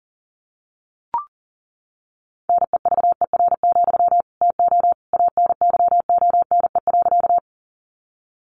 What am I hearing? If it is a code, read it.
Morse code, heard as a keyed tone, about 40 wpm, 700 Hz. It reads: NEVER, TO ANYONE.
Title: Sentences from Top 900 Words 40wpm